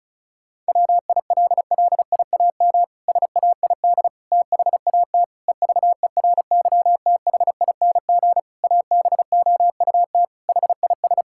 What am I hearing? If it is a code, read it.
Morse code, heard as a keyed tone, about 36 wpm, 700 Hz. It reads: WILLIAM SAID THAT EVERYTHING ABOUT HIS
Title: Pangram Challenge v2 - 35wpm